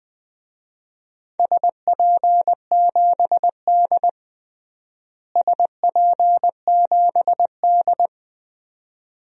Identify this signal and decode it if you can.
Morse code, heard as a keyed tone, about 20 wpm, 700 Hz. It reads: SP7D SP7D
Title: DXCC-100 Call Signs - 4 Characters - Repeated 2x - 20wpm